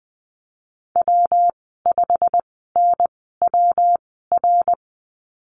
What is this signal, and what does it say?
Morse code, heard as a keyed tone, about 20 wpm, spaced as for 15 wpm, 700 Hz. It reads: W5NWR
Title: 193 Lesson 31g - Letter v - Call Signs 20wpm